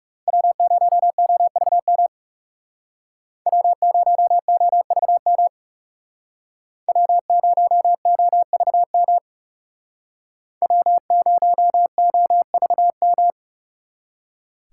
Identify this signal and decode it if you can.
Morse code, heard as a keyed tone, about 37 wpm, 700 Hz. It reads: W0OVM W0OVM W0OVM W0OVM
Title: Speed Racing - USA 1x3 Call Signs 30wpm